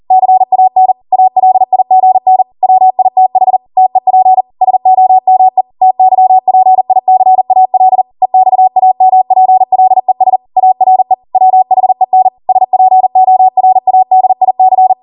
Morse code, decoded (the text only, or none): CAN ALIGN WITH TEP SOME TYPICAL EXAMPLES ARE WHEN SPORADIC